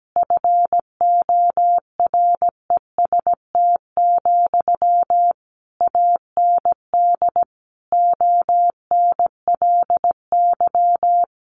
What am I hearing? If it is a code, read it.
Morse code, heard as a keyed tone, about 17 wpm, 700 Hz. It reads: FOREST, AND ONLY
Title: A Sound Of Thunder 17wpm